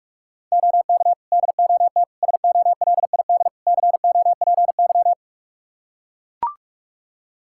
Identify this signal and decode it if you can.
Morse code, heard as a keyed tone, about 43 wpm, 700 Hz. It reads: OK DOT SOLID COPY
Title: Practice QSO - Exchange 3 - More Info - 45wpm